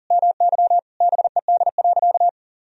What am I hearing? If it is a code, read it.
Morse code, heard as a keyed tone, about 40 wpm, 700 Hz. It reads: MY BED.
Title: Sentences from Top 1000 Words - Review Part 4 - 40wpm